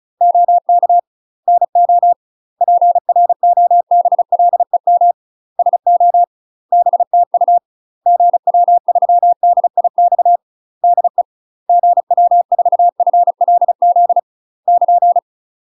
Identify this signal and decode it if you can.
Morse code, heard as a keyed tone, about 34 wpm, 700 Hz. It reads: OK NO PROBLEM SO BTU GW3DIX DE GW4FLZ <KN>
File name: UK MASTER QSOs TRAINING / 35WPM / UK MASTER QSOs 0015